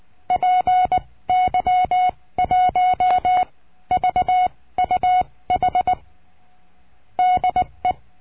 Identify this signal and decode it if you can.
Morse code, heard as a keyed tone, about 19 wpm, spaced as for 15 wpm, 730 Hz. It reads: PY1VUH DE
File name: cursos / CURSO DE TELEGRAFIA / Telegrafia - Trilha 22